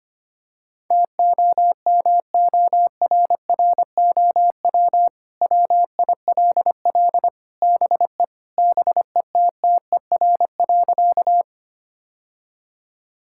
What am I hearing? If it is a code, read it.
Morse code, heard as a keyed tone, about 25 wpm, 700 Hz. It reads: TOMORROW WILL BE BETTER.
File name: practice / 25wpm / Sentences with 2-Word Phrases - 25wpm